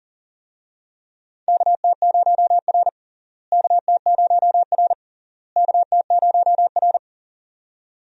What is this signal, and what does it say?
Morse code, heard as a keyed tone, about 37 wpm, 700 Hz. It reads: KT0R KT0R KT0R
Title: USA Call Signs - 4 Characters - Repeated 3x - 40wpm